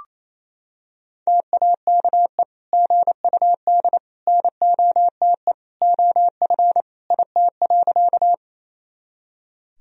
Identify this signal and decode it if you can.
Morse code, heard as a keyed tone, about 28 wpm, 700 Hz. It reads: TAKE GUD NOTE OF IT.
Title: Sentences from Top 500 Words and Abbreviations 28wpm